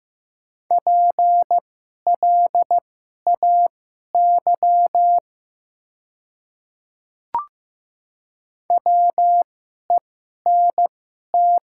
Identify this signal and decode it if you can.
Morse code, heard as a keyed tone, about 15 wpm, spaced as for 11 wpm, 700 Hz. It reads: PLAY WENT
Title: 2x Character Spacing - Top 200 Words - 15wpm